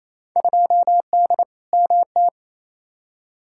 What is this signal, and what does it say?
Morse code, heard as a keyed tone, about 28 wpm, 700 Hz. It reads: 2D MT
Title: Field Day Exchanges Encore - 28wpm